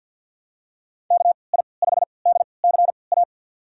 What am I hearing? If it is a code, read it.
Morse code, heard as a keyed tone, about 53 wpm, spaced as for 29 wpm, 700 Hz. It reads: KI5DCA
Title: USA Call Signs - 6 Characters - 3x Character Spacing - 50wpm